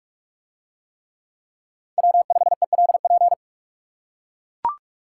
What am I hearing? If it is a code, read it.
Morse code, heard as a keyed tone, about 44 wpm, 700 Hz. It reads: W5ELP